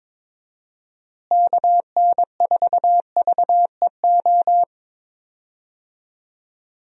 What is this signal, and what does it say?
Morse code, heard as a keyed tone, about 22 wpm, 700 Hz. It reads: KN4VEO
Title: USA Call Signs - 6 Characters - Repeated 3x - 22wpm